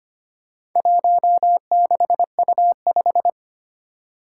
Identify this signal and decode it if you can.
Morse code, heard as a keyed tone, about 25 wpm, 700 Hz. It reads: 16U5
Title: Four Challenging Letter-Number Combinations 25wpm